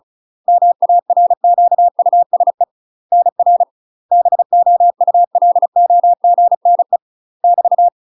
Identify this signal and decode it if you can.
Morse code, heard as a keyed tone, about 36 wpm, 700 Hz. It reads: MARQUSE NR BOULOGNE <BT>
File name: UK MASTER QSOs TRAINING / 35WPM / UK MASTER QSOs 0035